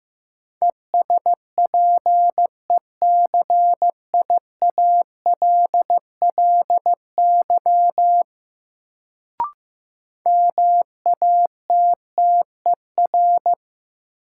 Morse code, heard as a keyed tone, about 15 wpm, 700 Hz. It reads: ESPECIALLY MATTER
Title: Sentences from Top 700 Words - New Word Drill - 15wpm